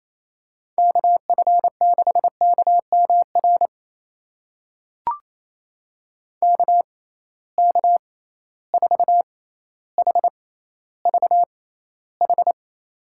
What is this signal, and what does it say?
Morse code, heard as a keyed tone, about 28 wpm, spaced as for 14 wpm, 700 Hz. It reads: KF6KMR KK4HVH
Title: USA Call Signs - 6 Characters - 6x Character Spacing - 28wpm